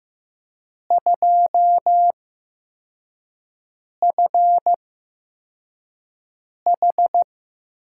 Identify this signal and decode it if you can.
Morse code, heard as a keyed tone, about 15 wpm, spaced as for 4 wpm, 700 Hz. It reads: 2FH
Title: USA Call Signs - 5 Characters - 8x Character Spacing - 15wpm